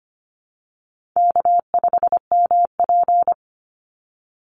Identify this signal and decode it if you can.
Morse code, heard as a keyed tone, about 25 wpm, 700 Hz. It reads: K5MP
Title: Easy Call Signs Repeated 3x - 25wpm